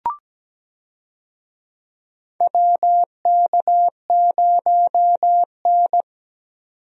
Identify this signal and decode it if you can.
Morse code, heard as a keyed tone, about 17 wpm, 700 Hz. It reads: WK0N
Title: USA Call Signs - 4 Characters - 17wpm